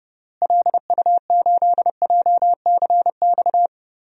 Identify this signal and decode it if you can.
Morse code, heard as a keyed tone, about 29 wpm, 700 Hz. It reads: LU8JCX
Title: DXCC-100 Call Signs - 6 Characters - Repeated 3x - 30wpm